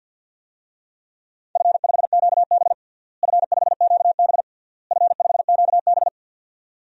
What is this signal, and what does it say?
Morse code, heard as a keyed tone, about 52 wpm, 700 Hz. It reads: F5QB F5QB F5QB
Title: DXCC-100 Call Signs - 4 Characters - Repeated 3x - 50wpm